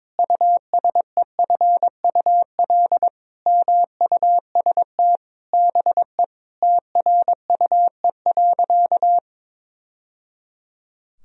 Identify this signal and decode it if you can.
Morse code, heard as a keyed tone, about 22 wpm, 700 Hz. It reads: USEFUL MUST BE TRUE.
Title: Sentences from Top 1000 Words - Review Part 4 - 22wpm